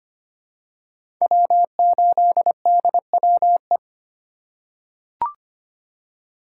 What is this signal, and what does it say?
Morse code, heard as a keyed tone, about 25 wpm, 700 Hz. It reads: W8DWE
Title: USA Call Signs - 4x Character Spacing - 25wpm